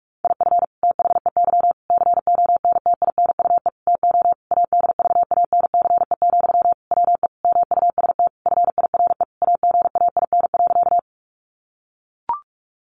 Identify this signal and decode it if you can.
Morse code, heard as a keyed tone, about 44 wpm, 700 Hz. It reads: IF THEY CONTINUE TO ADVANCE, WE MUST FIRE AGAIN.
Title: Sentences from Top 1400 Words 45wpm